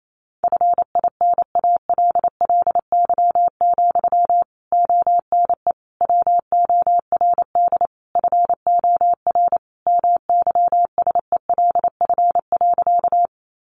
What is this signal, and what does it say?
Morse code, heard as a keyed tone, about 28 wpm, 700 Hz. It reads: FINALLY, ONE WORD FOR MYSELF.
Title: Sentences from Top 1300 Words 28wpm